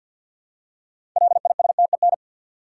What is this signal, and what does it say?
Morse code, heard as a keyed tone, about 50 wpm, 700 Hz. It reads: LISTEN
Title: Six Letter Words - 50wpm